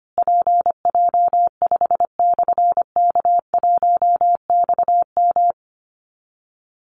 Morse code, heard as a keyed tone, about 25 wpm, 700 Hz. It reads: PJ5/K1XM
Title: Call Signs Repeated 3x - 25wpm